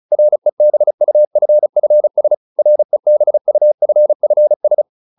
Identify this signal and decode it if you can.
Morse code, heard as a keyed tone, about 34 wpm, 600 Hz. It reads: REBUFFS REBUFFS